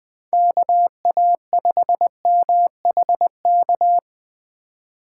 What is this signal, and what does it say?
Morse code, heard as a keyed tone, about 20 wpm, 700 Hz. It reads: KA5MHK
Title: USA Call Signs - 6 Characters - 3x Character Spacing - 20wpm